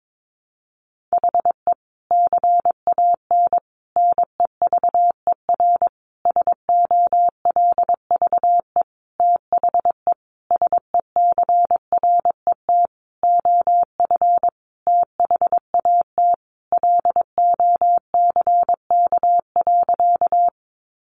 Morse code, heard as a keyed tone, about 22 wpm, 700 Hz. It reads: HE CAN NEVER SOLVE THE SECRET OF THAT LOCK.
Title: Sentences from Top 1300 Words 22wpm